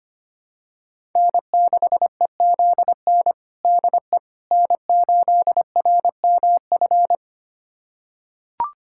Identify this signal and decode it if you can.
Morse code, heard as a keyed tone, about 25 wpm, 700 Hz. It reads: N6EZN DE N8RMF
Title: USA Callsign DE Callsign - 5 Characters - 25wpm